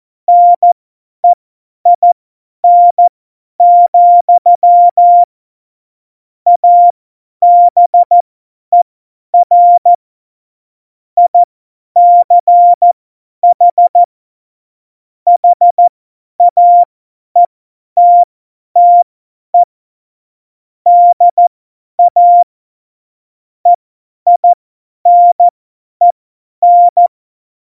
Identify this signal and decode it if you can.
Morse code, heard as a keyed tone, about 14 wpm, spaced as for 10 wpm, 700 Hz. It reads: NEIN, ABER ICH HAETTE DA EINEN
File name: Morsekurs / Witze / Witz 06. November 2018, 50 F70